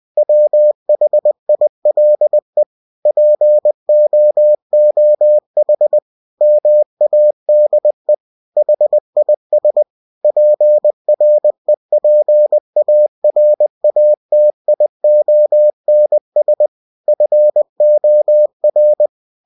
Morse code, heard as a keyed tone, about 20 wpm, 600 Hz. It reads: WHILE POOH MADE HIS PREPARATIONS FOR
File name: cw-books / winnie-the-pooh / winnie-20wpm-page-0038